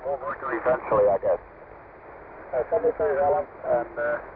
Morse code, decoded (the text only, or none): none